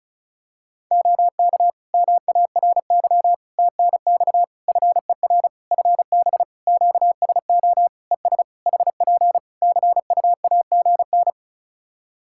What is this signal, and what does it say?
Morse code, heard as a keyed tone, about 36 wpm, 700 Hz. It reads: OK MARY TNX FER FB QSO ES HP CUAGN
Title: Practice QSO - Exchange 4 - Ending - 35wpm